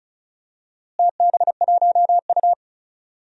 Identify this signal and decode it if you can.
Morse code, heard as a keyed tone, about 36 wpm, 700 Hz. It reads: TB1U
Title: DXCC-100 Call Signs - 4 Characters - Repeated 2x - 35wpm